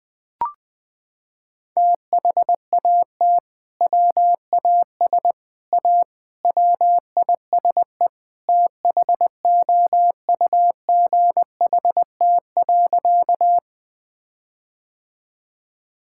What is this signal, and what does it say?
Morse code, heard as a keyed tone, about 20 wpm, 700 Hz. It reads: THAT WAS A WISE THOUGHT.
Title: Sentences from Top 1600 Words 20wpm